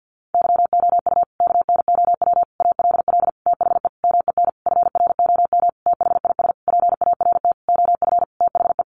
Morse code, heard as a keyed tone, about 50 wpm, 700 Hz. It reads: YOU KNOW ALL THE MEN FROM THIS PART OF THE
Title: Sentences from Top 300 Words 50wpm